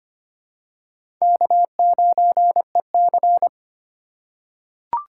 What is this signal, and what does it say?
Morse code, heard as a keyed tone, about 25 wpm, 700 Hz. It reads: K9EC
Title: USA Call Signs - 4 Characters - 2x Character Spacing - 25wpm